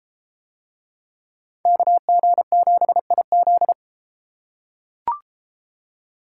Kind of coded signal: Morse code

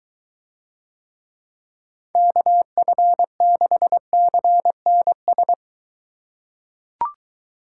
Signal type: Morse code

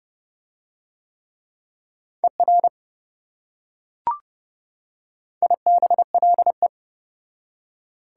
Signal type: Morse code

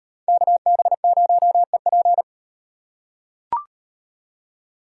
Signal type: Morse code

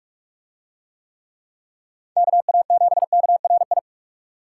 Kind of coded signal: Morse code

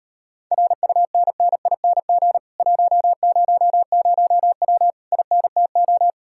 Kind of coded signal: Morse code

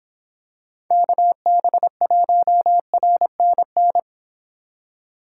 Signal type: Morse code

